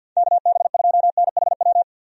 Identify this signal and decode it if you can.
Morse code, heard as a keyed tone, about 52 wpm, 700 Hz. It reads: KB2NHW
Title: USA Call Signs - 6 Characters - 50wpm